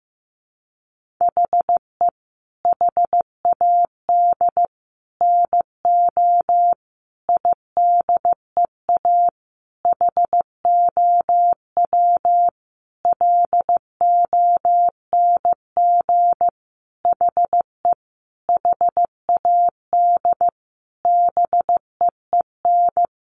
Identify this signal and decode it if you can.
Morse code, heard as a keyed tone, about 15 wpm, 700 Hz. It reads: HE HAD NO IDEA HOW LONG HE HAD BEEN
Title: Sentences from Top 300 Words 15wpm